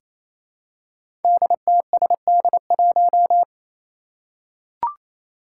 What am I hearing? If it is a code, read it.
Morse code, heard as a keyed tone, about 28 wpm, 700 Hz. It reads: DTSD1
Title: Five Challenging Letter-Number Combinations 28wpm